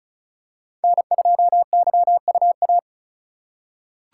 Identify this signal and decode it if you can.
Morse code, heard as a keyed tone, about 36 wpm, 700 Hz. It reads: N2YUA
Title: USA Call Signs - 5 Characters - 4x Character Spacing - 35wpm